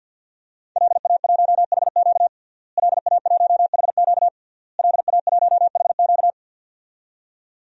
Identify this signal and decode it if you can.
Morse code, heard as a keyed tone, about 47 wpm, 700 Hz. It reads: LA1HQ LA1HQ LA1HQ
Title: DXCC-100 Call Signs - 5 Characters - Repeated 3x - 50wpm